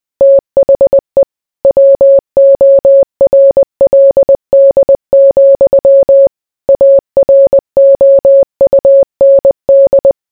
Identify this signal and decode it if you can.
Morse code, heard as a keyed tone, about 20 wpm, 550 Hz. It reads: THE WORLD, AROUND